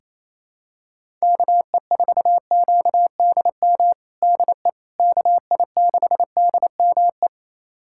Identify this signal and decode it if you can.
Morse code, heard as a keyed tone, about 28 wpm, 700 Hz. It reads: KE4QDM DE KI6DME
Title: USA Callsign DE Callsign - 6 Characters - 28wpm